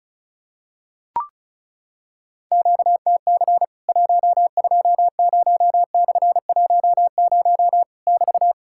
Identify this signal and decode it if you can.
Morse code, heard as a keyed tone, about 34 wpm, 700 Hz. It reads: QTC 120/10 <BT>
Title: Worked All Europe DX Contest - QTC Exchange - 35wpm